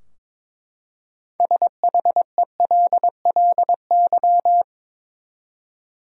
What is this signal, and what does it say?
Morse code, heard as a keyed tone, about 22 wpm, 700 Hz. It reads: SHELLY
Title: USA Common Names - 22wpm